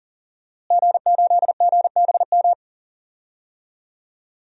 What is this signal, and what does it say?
Morse code, heard as a keyed tone, about 38 wpm, 700 Hz. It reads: G8GBM
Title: DXCC-100 Call Signs - 5 Characters - Repeated 2x - 40wpm